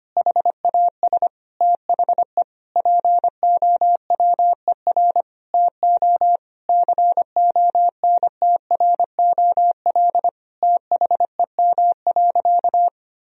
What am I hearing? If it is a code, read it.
Morse code, heard as a keyed tone, about 25 wpm, 700 Hz. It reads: HAS THE POWER TO CONTROL THEM.
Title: Sentences from Top 1000 Words - Review Part 2 - 25wpm